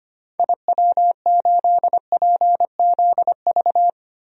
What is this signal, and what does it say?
Morse code, heard as a keyed tone, about 25 wpm, 700 Hz. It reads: IW8PZV
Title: DXCC-100 Call Signs - 6 Characters - Repeated 3x - 25wpm